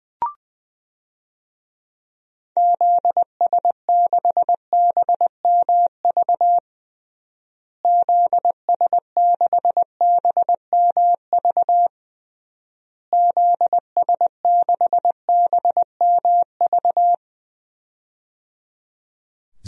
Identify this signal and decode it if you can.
Morse code, heard as a keyed tone, about 21 wpm, 700 Hz. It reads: ZS6BMV ZS6BMV ZS6BMV
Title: DXCC-100 Call Signs - 6 Characters - Repeated 3x - 20wpm